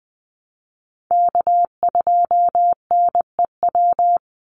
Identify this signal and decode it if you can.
Morse code, heard as a keyed tone, about 20 wpm, 700 Hz. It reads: K2NEW
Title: USA Call Signs 20wpm